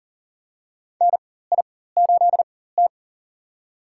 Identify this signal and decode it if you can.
Morse code, heard as a keyed tone, about 40 wpm, spaced as for 19 wpm, 700 Hz. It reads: NI8T
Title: USA Call Signs - 4 Characters - 4x Character Spacing - 40wpm